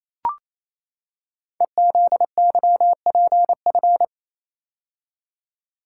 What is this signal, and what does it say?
Morse code, heard as a keyed tone, about 27 wpm, 700 Hz. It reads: EZYPF
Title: Code Groups - 5 Letters - 28wpm